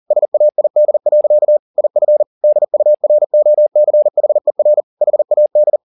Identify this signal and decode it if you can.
Morse code, heard as a keyed tone, about 40 wpm, 600 Hz. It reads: SAID. IF DUROCHER HAD